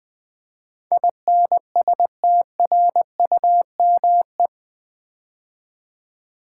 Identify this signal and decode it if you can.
Morse code, heard as a keyed tone, about 20 wpm, 700 Hz. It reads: INSTRUME
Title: Build A Word - 4 Letters At A Time - 12-Letter Words - 20wpm